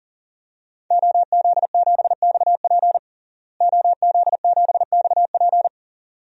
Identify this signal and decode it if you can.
Morse code, heard as a keyed tone, about 40 wpm, 700 Hz. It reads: OZ7XP OZ7XP
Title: DXCC-100 Call Signs - 5 Characters - Repeated 2x - 40wpm